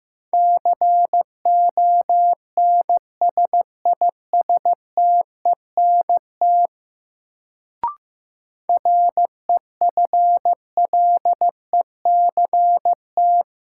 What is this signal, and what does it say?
Morse code, heard as a keyed tone, about 15 wpm, 700 Hz. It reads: CONSISTENT REFLECT
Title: Sentences from Top 1500 Words - New Word Drill - 15wpm